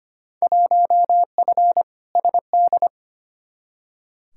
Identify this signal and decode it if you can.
Morse code, heard as a keyed tone, about 25 wpm, 700 Hz. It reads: 1F SD
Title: Field Day Exchanges - 25wpm